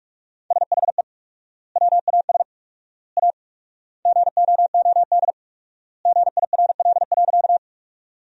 Morse code, heard as a keyed tone, about 46 wpm, 700 Hz. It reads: SHE WAS A GOOD GIRL.
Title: 4x Word-Spacing - Sentences from Top 400 Words - 45wpm